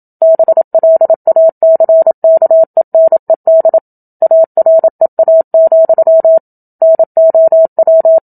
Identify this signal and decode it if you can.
Morse code, heard as a keyed tone, about 27 wpm, 650 Hz. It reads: BLACKENED AREA, NOW
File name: Book Two / 01-Chapter 1 - 27 WPM